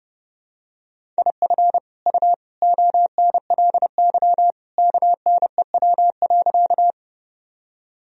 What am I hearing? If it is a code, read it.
Morse code, heard as a keyed tone, about 30 wpm, 700 Hz. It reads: IF U ONLY KNEW.
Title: Sentences from Top 400 Words and Abbreviations 30wpm